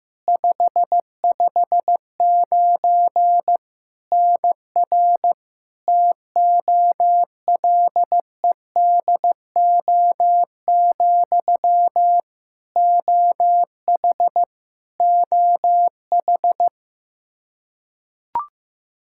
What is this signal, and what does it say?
Morse code, heard as a keyed tone, about 15 wpm, 700 Hz. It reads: 559 NR TOLEDO, OH OH
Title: Practice QSO - Exchange 1 - Beginning - 15wpm